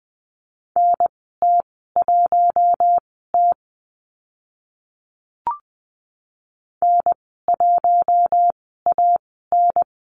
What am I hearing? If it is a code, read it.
Morse code, heard as a keyed tone, about 20 wpm, spaced as for 15 wpm, 700 Hz. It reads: NT1T N1AN
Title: 042 Lesson 8g - Number 1 - Call Signs 20wpm